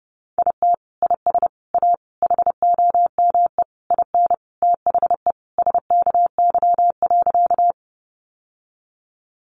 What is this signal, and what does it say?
Morse code, heard as a keyed tone, about 30 wpm, 700 Hz. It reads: IT IS A HOME IN THE SKY.
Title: Sentences from Top 1200 Words 30wpm